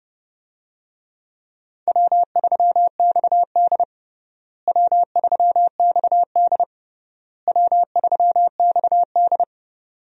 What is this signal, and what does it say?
Morse code, heard as a keyed tone, about 30 wpm, 700 Hz. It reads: W3XD W3XD W3XD
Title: USA Call Signs - 4 Characters - Repeated 3x - 30wpm